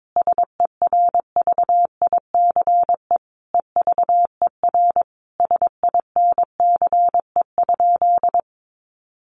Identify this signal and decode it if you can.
Morse code, heard as a keyed tone, about 22 wpm, 700 Hz. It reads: SERVICE EVER SINCE?
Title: Sentences from Top 700 Words 22wpm